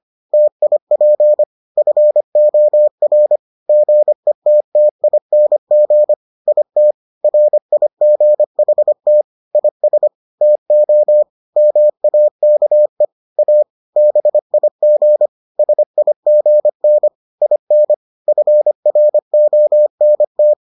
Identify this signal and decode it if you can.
Morse code, heard as a keyed tone, about 25 wpm, 600 Hz. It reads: TIP FOR GETTING IT RIGHT IS TO MAKE A BIG SIGN IN FRONT